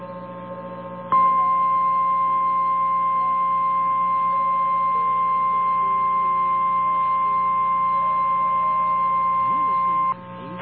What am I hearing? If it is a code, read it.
Emergency Alert System (SAME) tones (the 1050 Hz attention tone). It is heard from 1.1 to 10.2 s.
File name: eas / KNCYKBIE / KNCYKBIE